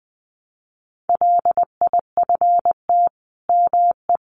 Morse code, heard as a keyed tone, about 19 wpm, 700 Hz. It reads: LIFT ME
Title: Sentences from Top 1000 Words 20wpm